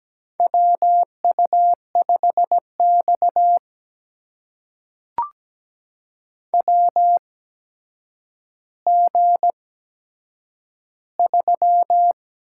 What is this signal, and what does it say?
Morse code, heard as a keyed tone, about 17 wpm, spaced as for 8 wpm, 700 Hz. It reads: WU5X WG3
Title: USA Call Signs - 4 Characters - 8x Character Spacing - 17wpm